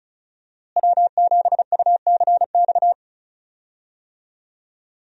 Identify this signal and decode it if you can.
Morse code, heard as a keyed tone, about 36 wpm, 700 Hz. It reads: W7UCX
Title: USA Call Signs - 5 Characters - Repeated 3x - 35wpm